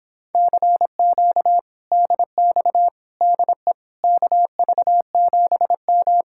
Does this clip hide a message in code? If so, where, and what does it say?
Yes: Morse code from 0.3 to 6.2 s, heard as a keyed tone, about 26 wpm, 700 Hz. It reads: CQ DX DE KV7M